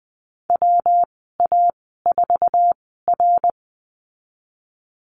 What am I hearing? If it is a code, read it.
Morse code, heard as a keyed tone, about 20 wpm, spaced as for 14 wpm, 700 Hz. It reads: WA4R
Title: 154 Lesson 25g - Letter p - Call Signs 20wpm